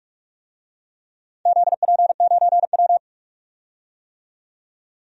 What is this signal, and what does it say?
Morse code, heard as a keyed tone, about 44 wpm, 700 Hz. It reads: ZP9W
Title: DXCC-100 Call Signs - 4 Characters - 45wpm